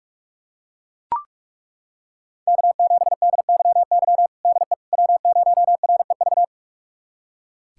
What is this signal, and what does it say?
Morse code, heard as a keyed tone, about 42 wpm, 700 Hz. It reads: K7DYY DE W0REV
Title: USA Callsign DE Callsign - 5 Characters - 45wpm